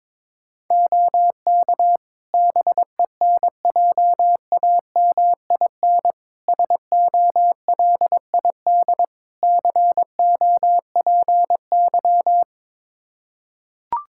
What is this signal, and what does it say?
Morse code, heard as a keyed tone, about 22 wpm, 700 Hz. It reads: OK BENJAMIN SOLID COPY
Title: K1USN SST QSO Practice 22wpm